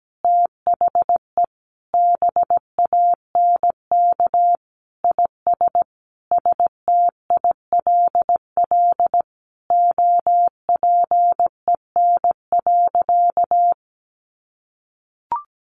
Morse code, heard as a keyed tone, about 17 wpm, 700 Hz. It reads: THE BANK IS STILL OPEN.
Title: Sentences from Top 1500 Words 17wpm